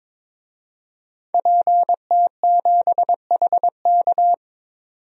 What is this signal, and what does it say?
Morse code, heard as a keyed tone, about 22 wpm, 700 Hz. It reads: PT7HK
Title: DXCC-100 Call Signs - 5 Characters - 22wpm